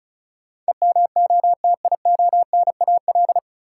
Morse code, heard as a keyed tone, about 36 wpm, 700 Hz. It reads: EMOTIONAL
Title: Sentences from Top 1000 Words - Review Part 5 - New Word Drill - 35wpm